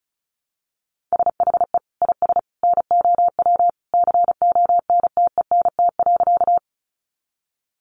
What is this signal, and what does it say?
Morse code, heard as a keyed tone, about 34 wpm, 700 Hz. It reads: SHE IS NOW CONTENT.
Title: Sentences from Top 900 Words 35wpm